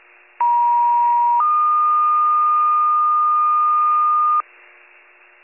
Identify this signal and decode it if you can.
Two-tone page, heard as a one-second tone, then a three-second tone, about 950 then 1220 Hz, with noise. It is heard from 0.4 to 4.4 s.